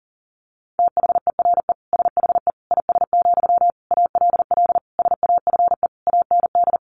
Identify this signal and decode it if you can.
Morse code, heard as a keyed tone, about 40 wpm, 700 Hz. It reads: THERE SHE IS, ALL SAFE AND
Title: Sentences from Top 900 Words 40wpm